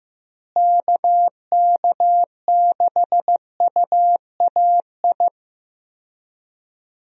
Morse code, heard as a keyed tone, about 15 wpm, 700 Hz. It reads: KK6UAI
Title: USA Call Signs - 6 Characters - Repeated 3x - 15wpm